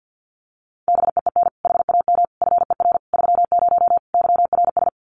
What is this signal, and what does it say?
Morse code, heard as a keyed tone, about 49 wpm, 700 Hz. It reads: BEEN HAM FER 30 YRS
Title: Common QSO Phrases Encore 50wpm